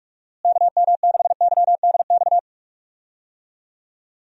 Morse code, heard as a keyed tone, about 46 wpm, 700 Hz. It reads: KM6YDX
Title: USA Call Signs - 6 Characters - Repeated 3x - 45wpm